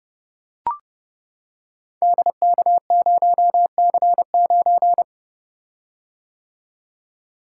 Morse code, heard as a keyed tone, about 31 wpm, 700 Hz. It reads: DK0C9
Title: Code Groups - 5 Letter:Numbers - 30wpm